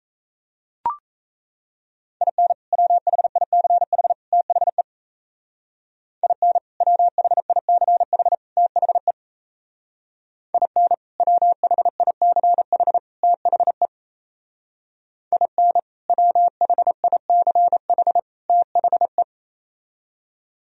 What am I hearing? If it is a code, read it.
Morse code, heard as a keyed tone, about 37 wpm, 700 Hz. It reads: IN WHICH THE IN WHICH THE IN WHICH THE IN WHICH THE